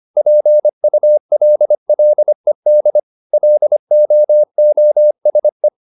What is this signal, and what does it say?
Morse code, heard as a keyed tone, about 25 wpm, 600 Hz. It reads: PULLED LOOSE